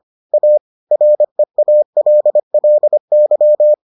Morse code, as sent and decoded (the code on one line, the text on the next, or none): .- / .-. .- .-.. .-.. -.--
A REALLY